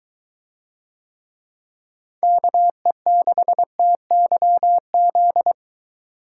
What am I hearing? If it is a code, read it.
Morse code, heard as a keyed tone, about 22 wpm, 700 Hz. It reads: KE6TYZ